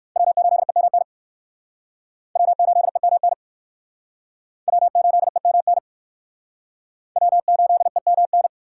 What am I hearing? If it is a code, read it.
Morse code, heard as a keyed tone, about 60 wpm, 700 Hz. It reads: W8EMN W8EMN W8EMN W8EMN